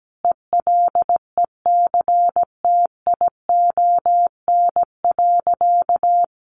Morse code, heard as a keyed tone, about 17 wpm, 700 Hz. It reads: ELECTION.
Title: Sentences from Top 1500 Words 17wpm